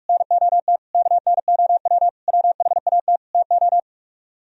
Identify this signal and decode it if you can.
Morse code, heard as a keyed tone, about 44 wpm, 700 Hz. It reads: NOT KNOW WHAT TO